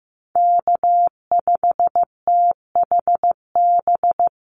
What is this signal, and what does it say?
Morse code, heard as a keyed tone, about 15 wpm, 700 Hz. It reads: K5THB
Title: USA Call Signs 15wpm